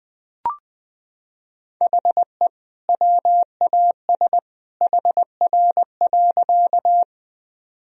Morse code, heard as a keyed tone, about 20 wpm, 700 Hz. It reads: HE WAS HR.
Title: Sentences from Top 200 Words and Abbreviations 20wpm